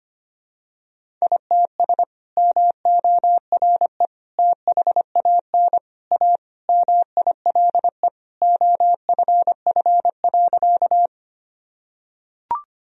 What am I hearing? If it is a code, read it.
Morse code, heard as a keyed tone, about 25 wpm, 700 Hz. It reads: ITS MORE THAN A MILE OFF.
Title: Sentences from Top 1000 Words - Review Part 1 - 25wpm